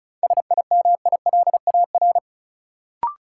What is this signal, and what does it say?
Morse code, heard as a keyed tone, about 36 wpm, 700 Hz. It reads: SIMILAR
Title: Sentences from Top 1000 Words - Review Part 5 - New Word Drill - 35wpm